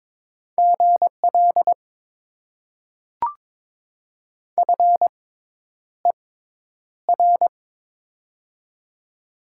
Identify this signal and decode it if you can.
Morse code, heard as a keyed tone, about 22 wpm, spaced as for 10 wpm, 700 Hz. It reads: GL FER
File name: Top 90 QSO Elements - 6x Character Spacing - 22wpm